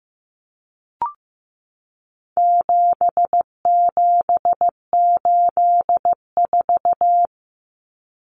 Morse code, heard as a keyed tone, about 15 wpm, 700 Hz. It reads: 7784
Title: Common Cut Numbers - 4-Digit 15wpm